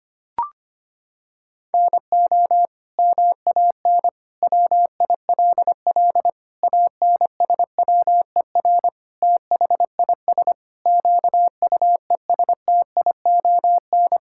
Morse code, heard as a keyed tone, about 25 wpm, 700 Hz. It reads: NO MAN WILL ANSWER THIS QUESTION
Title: Sentences from Top 1000 Words - Review Part 6 - 25wpm